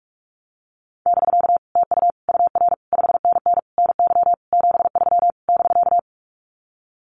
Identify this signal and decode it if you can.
Morse code, heard as a keyed tone, about 46 wpm, 700 Hz. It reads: <BK> TU UR 5NN NY 73 <BK>
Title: Common QSO Phrases Encore 45wpm